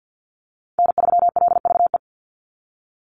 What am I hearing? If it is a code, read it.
Morse code, heard as a keyed tone, about 50 wpm, 700 Hz. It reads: N3LVE
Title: USA Call Signs 50wpm